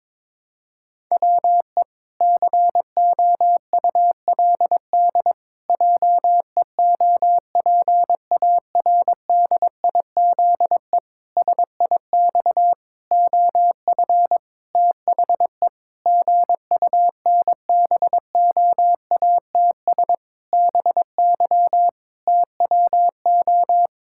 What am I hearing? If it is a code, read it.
Morse code, heard as a keyed tone, about 22 wpm, 700 Hz. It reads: WE COULD JEOPARDIZE SIX OF THE GUNBOATS BY TWO
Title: Pangram Challenge v2 - 22wpm